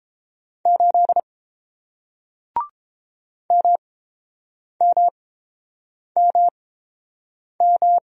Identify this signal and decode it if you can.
Morse code, heard as a keyed tone, about 30 wpm, spaced as for 7 wpm, 700 Hz. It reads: 8 MMMM